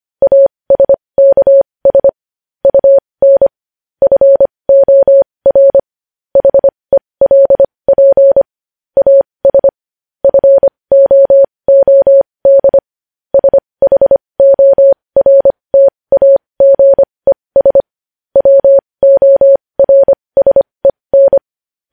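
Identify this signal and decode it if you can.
Morse code, heard as a keyed tone, about 25 wpm, spaced as for 20 wpm, 550 Hz. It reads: ASKS UN FOR HELP AS FOOD SHORTAGES WORSEN
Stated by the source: Morse code news headlines sent at 20 wpm.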